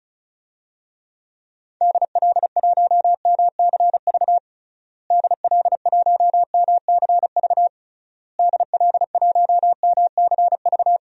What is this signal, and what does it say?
Morse code, heard as a keyed tone, about 36 wpm, 700 Hz. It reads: DL1MCV DL1MCV DL1MCV
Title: DXCC-100 Call Signs - 6 Characters - Repeated 3x - 35wpm